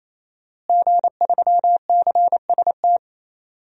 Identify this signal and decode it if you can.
Morse code, heard as a keyed tone, about 28 wpm, 700 Hz. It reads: G3CST
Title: DXCC-100 Call Signs - 5 Characters - Repeated 3x - 28wpm